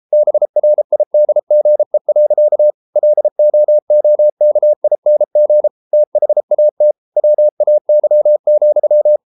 Morse code, heard as a keyed tone, about 33 wpm, 600 Hz. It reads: BRIDGE. LOOKING THAT WAY,